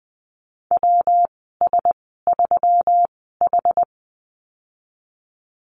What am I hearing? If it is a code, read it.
Morse code, heard as a keyed tone, about 20 wpm, spaced as for 15 wpm, 700 Hz. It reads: WS3H
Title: 122 Lesson 20g - Number 3 - Call Signs 20wpm